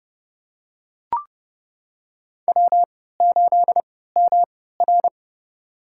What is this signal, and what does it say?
Morse code, heard as a keyed tone, about 30 wpm, spaced as for 17 wpm, 700 Hz. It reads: W8MR
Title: USA Call Signs - 4 Characters - 3x Character Spacing - 30wpm